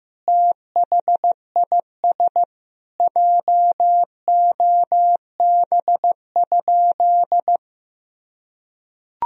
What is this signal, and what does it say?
Morse code, heard as a keyed tone, about 16 wpm, 700 Hz. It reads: THIS JOB?
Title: Sentences from Top 1000 Words - Review Part 3 - 15wpm